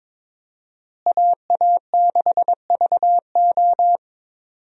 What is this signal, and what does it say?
Morse code, heard as a keyed tone, about 22 wpm, 700 Hz. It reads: AA6VO
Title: USA Call Signs - 5 Characters - Repeated 2x - 22wpm